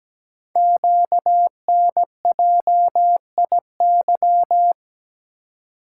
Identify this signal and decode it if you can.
Morse code, heard as a keyed tone, about 17 wpm, 700 Hz. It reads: QNJIY
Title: Code Groups - 5 Letters - 17wpm